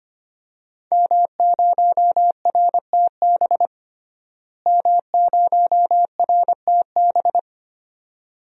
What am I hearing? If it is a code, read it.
Morse code, heard as a keyed tone, about 25 wpm, 700 Hz. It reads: M0RTB M0RTB
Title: DXCC-100 Call Signs - 5 Characters - Repeated 2x - 25wpm